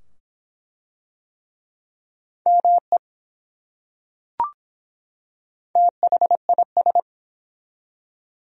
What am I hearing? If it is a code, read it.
Morse code, heard as a keyed tone, about 24 wpm, 700 Hz. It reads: ME THIS